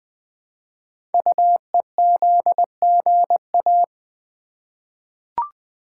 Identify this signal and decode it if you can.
Morse code, heard as a keyed tone, about 20 wpm, 700 Hz. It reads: UEZGA
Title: Code Groups - 5 Letter:Numbers - 20wpm